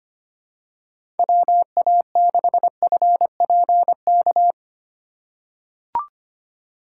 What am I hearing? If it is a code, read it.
Morse code, heard as a keyed tone, about 25 wpm, 700 Hz. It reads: WA6FPK
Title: USA Call Signs - 6 Characters - 3x Character Spacing - 25wpm